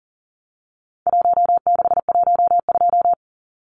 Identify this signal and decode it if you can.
Morse code, heard as a keyed tone, about 40 wpm, 700 Hz. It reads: 1612
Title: Numbers - Contest Style - 4-Digit 40wpm